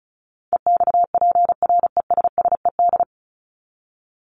Morse code, heard as a keyed tone, about 34 wpm, 700 Hz. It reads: EXPRESSED
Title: ED Suffix - Top 100 Words with Base Words 35wpm